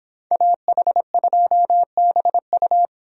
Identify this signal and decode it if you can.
Morse code, heard as a keyed tone, about 26 wpm, 700 Hz. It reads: AH2BU